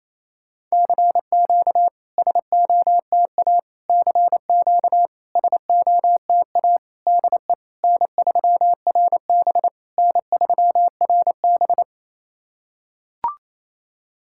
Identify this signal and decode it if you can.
Morse code, heard as a keyed tone, about 28 wpm, 690 Hz. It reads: CQ SOTA CQ SOTA DE N3RB N3RB
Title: SOTA QSO Practice 28wpm